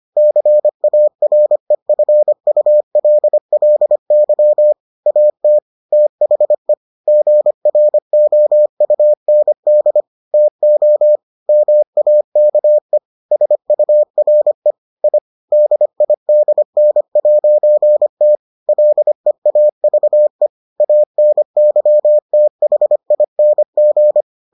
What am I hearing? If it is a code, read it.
Morse code, heard as a keyed tone, about 25 wpm, 600 Hz. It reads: CAREFULLY AT THE GROUND TO MAKE SURE I DIDN'T LEAVE ANYTHING